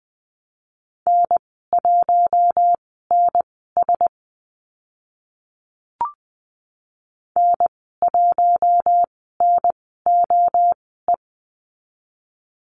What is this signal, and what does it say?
Morse code, heard as a keyed tone, about 20 wpm, spaced as for 14 wpm, 700 Hz. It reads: N1NS N1NOE
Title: 042 Lesson 8g - Number 1 - Call Signs 20wpm